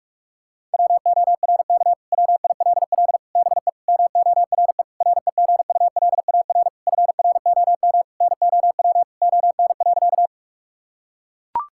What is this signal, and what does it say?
Morse code, heard as a keyed tone, about 41 wpm, 700 Hz. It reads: WORK WILL BE MORE REGULAR FROM NOW ON.
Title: Sentences from Top 1000 Words - Review Part 6 - 45wpm